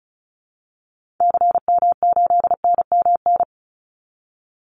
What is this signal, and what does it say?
Morse code, heard as a keyed tone, about 34 wpm, 700 Hz. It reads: CM8NMN
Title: Call Signs Repeated 2x - 35wpm